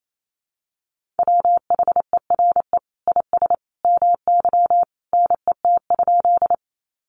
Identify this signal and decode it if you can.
Morse code, heard as a keyed tone, about 28 wpm, 700 Hz. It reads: WHERE IS MY NET?
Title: Sentences from Top 1400 Words 28wpm